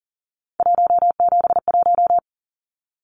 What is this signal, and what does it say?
Morse code, heard as a keyed tone, about 40 wpm, 700 Hz. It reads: 171
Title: Numbers - 3-Digit 40wpm